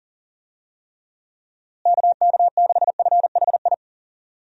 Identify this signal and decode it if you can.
Morse code, heard as a keyed tone, about 40 wpm, 700 Hz. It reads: KK6FHI
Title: USA Call Signs - 6 Characters - Repeated 2x - 40wpm